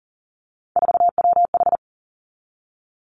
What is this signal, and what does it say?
Morse code, heard as a keyed tone, about 40 wpm, 700 Hz. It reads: 4WH
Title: Three Challenging Letter-Number Combinations 40wpm